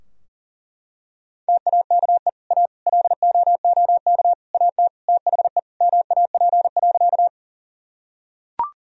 Morse code, heard as a keyed tone, about 40 wpm, 700 Hz. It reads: TAKE A LOOK AT THE MAP.
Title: Sentences from Top 1000 Words - Review Part 2 - 40wpm